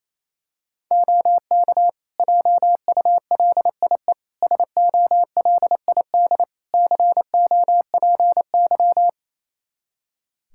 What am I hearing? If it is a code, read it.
Morse code, heard as a keyed tone, about 28 wpm, 700 Hz. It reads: OK JULIE SOLID COPY
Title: Practice QSO - Exchange 3 - More Info - 28wpm